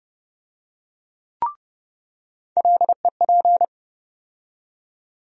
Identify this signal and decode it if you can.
Morse code, heard as a keyed tone, about 30 wpm, 700 Hz. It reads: LEP
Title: USA Call Sign Postfixes - 3 Letters 30wpm